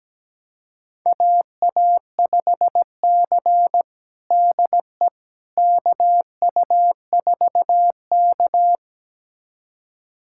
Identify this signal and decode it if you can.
Morse code, heard as a keyed tone, about 17 wpm, 700 Hz. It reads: AA5C DE KU4K
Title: USA Callsign DE Callsign - 4 Characters - 17wpm